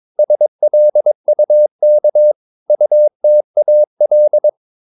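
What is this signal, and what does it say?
Morse code, heard as a keyed tone, about 22 wpm, 600 Hz. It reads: SLUK UTAL
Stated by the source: Morse practice of short Danish words at 22 wpm, from CW med Gnister.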